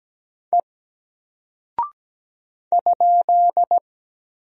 Morse code, heard as a keyed tone, about 17 wpm, 700 Hz. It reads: E ?
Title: Single Character (Letter:Number:Punctuation:Prosign) Alt 17wpm